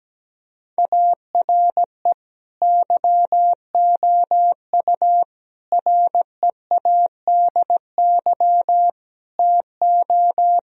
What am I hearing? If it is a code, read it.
Morse code, heard as a keyed tone, about 17 wpm, 700 Hz. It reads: ARE YOU READY TO
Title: Sentences from Top 1000 Words - Review Part 1 - 17wpm